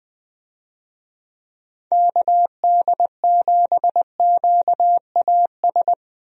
Morse code, heard as a keyed tone, about 20 wpm, 700 Hz. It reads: KD7QAS
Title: USA Call Signs - 6 Characters - 20wpm